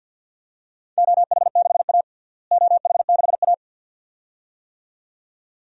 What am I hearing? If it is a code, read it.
Morse code, heard as a keyed tone, about 47 wpm, 700 Hz. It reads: OH6A OH6A
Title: DXCC-100 Call Signs - 4 Characters - Repeated 2x - 50wpm